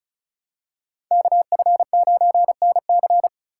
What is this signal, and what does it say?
Morse code, heard as a keyed tone, about 34 wpm, 700 Hz. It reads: KF9NC
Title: USA Call Signs - 5 Characters - 4x Character Spacing - 35wpm